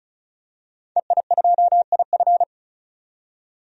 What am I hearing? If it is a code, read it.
Morse code, heard as a keyed tone, about 34 wpm, 700 Hz. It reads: EI2IF
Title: DXCC-100 Call Signs - 5 Characters - 35wpm